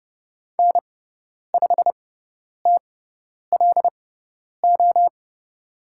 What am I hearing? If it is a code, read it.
Morse code, heard as a keyed tone, about 30 wpm, spaced as for 10 wpm, 700 Hz. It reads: N5TLO
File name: Lesson 17g - Letter u - Call Signs 30wpm